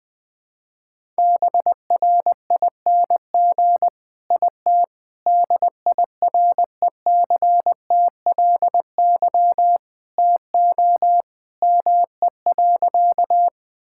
Morse code, heard as a keyed tone, about 20 wpm, 700 Hz. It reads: BRING IT DIRECTLY TO ME.
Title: Sentences from Top 1000 Words - Review Part 5 - 20wpm